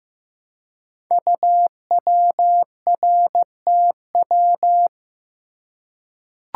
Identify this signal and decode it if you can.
Morse code, heard as a keyed tone, about 15 wpm, 700 Hz. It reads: UWRTW
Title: Code Groups - 5 Letters - 15wpm